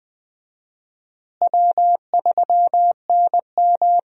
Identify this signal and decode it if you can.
Morse code, heard as a keyed tone, about 20 wpm, 700 Hz. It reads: W3NM
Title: USA Call Signs - 4 Characters - Repeated 3x - 20wpm